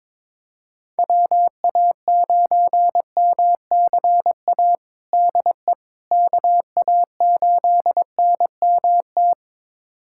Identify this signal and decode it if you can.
Morse code, heard as a keyed tone, about 22 wpm, 700 Hz. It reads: WA9MCA DE KA8NMT
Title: USA Callsign DE Callsign - 6 Characters - 22wpm